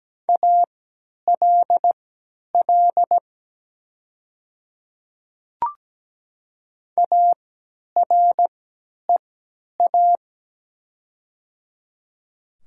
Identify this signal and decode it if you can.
Morse code, heard as a keyed tone, about 17 wpm, spaced as for 10 wpm, 700 Hz. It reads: ALL AREA
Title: 3x Character Spacing - Top 500 Words - 17wpm